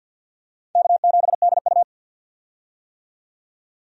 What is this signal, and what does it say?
Morse code, heard as a keyed tone, about 52 wpm, 700 Hz. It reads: K7DU
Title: USA Call Signs - 4 Characters - 50wpm